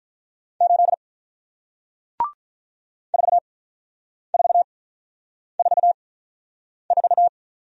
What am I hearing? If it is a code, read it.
Morse code, heard as a keyed tone, about 46 wpm, spaced as for 9 wpm, 700 Hz. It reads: 8 4444